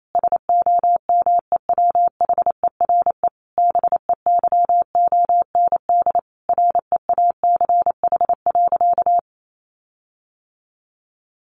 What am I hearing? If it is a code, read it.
Morse code, heard as a keyed tone, about 28 wpm, 700 Hz. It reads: SOMEWHERE BEYOND REACH.
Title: Sentences from Top 1400 Words 28wpm